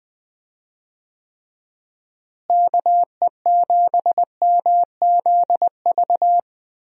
Morse code, heard as a keyed tone, about 20 wpm, 700 Hz. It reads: KE7MZV
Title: USA Call Signs - 6 Characters - Repeated 2x - 20wpm